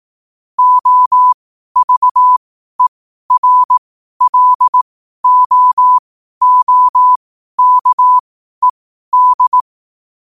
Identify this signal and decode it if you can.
Morse code, heard as a keyed tone, about 18 wpm, spaced as for 12 wpm, 1000 Hz. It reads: OVERLOOKED
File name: qotdfiles / 12wpm